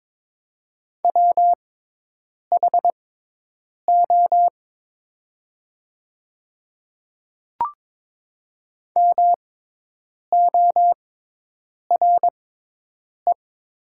Morse code, heard as a keyed tone, about 22 wpm, spaced as for 8 wpm, 700 Hz. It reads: WHO MORE
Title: 6x Character Spacing - Top 100 Words - 22wpm